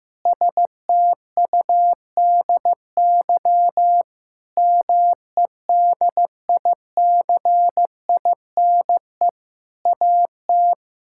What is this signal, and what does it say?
Morse code, heard as a keyed tone, about 15 wpm, 700 Hz. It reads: STUDY MEDICINE AT
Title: Sentences from Top 1000 Words - Review Part 6 - 15wpm